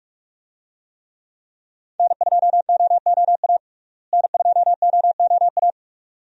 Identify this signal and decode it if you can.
Morse code, heard as a keyed tone, about 46 wpm, 700 Hz. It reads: N2OOA N2OOA
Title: USA Call Signs - 5 Characters - Repeated 2x - 45wpm